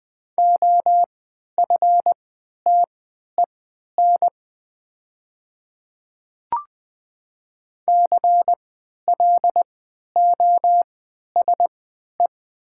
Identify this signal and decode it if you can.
Morse code, heard as a keyed tone, about 20 wpm, spaced as for 11 wpm, 700 Hz. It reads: OFTEN CLOSE
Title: 3x Character Spacing - Top 300 Words - 20wpm